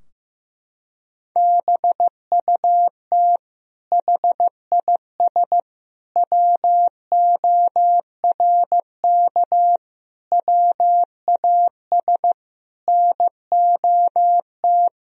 Morse code, heard as a keyed tone, about 15 wpm, 700 Hz. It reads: BUT HIS WORK WAS NOT
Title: Sentences from Top 1000 Words - Review Part 2 - 15wpm